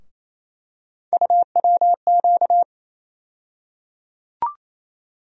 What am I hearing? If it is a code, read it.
Morse code, heard as a keyed tone, about 27 wpm, 700 Hz. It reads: UWQ
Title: USA Call Sign Postfixes - 3 Letters 28wpm